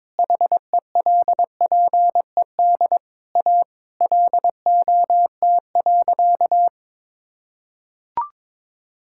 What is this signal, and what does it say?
Morse code, heard as a keyed tone, about 22 wpm, 700 Hz. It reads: HELPED A LOT.
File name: practice / 22wpm / Sentences with 3-Word Phrases - 22wpm